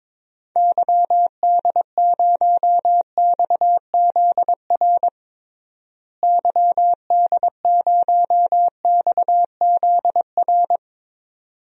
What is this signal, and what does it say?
Morse code, heard as a keyed tone, about 22 wpm, 700 Hz. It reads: YD0XZR YD0XZR
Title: DXCC-100 Call Signs - 6 Characters - Repeated 2x - 22wpm